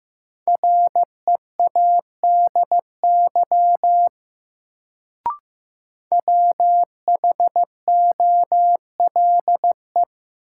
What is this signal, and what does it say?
Morse code, heard as a keyed tone, about 16 wpm, 700 Hz. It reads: READY WHOLE
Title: Sentences from Top 400 Words - New Word Drill - 15wpm